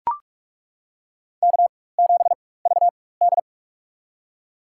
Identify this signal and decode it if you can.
Morse code, heard as a keyed tone, about 42 wpm, spaced as for 21 wpm, 700 Hz. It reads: K7VD
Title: USA Call Signs - 4 Characters - 4x Character Spacing - 45wpm